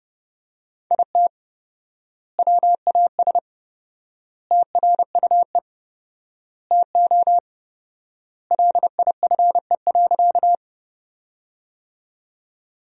Morse code, heard as a keyed tone, about 30 wpm, 700 Hz. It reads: IT WAS TRUE TO LIFE.
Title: 4x Word-Spacing - Sentences from Top 400 Words - 30wpm